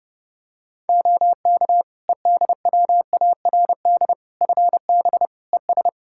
Morse code, heard as a keyed tone, about 30 wpm, 700 Hz. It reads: OK EDWARD FB ES
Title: Practice QSO - Exchange 2 - Info - 30wpm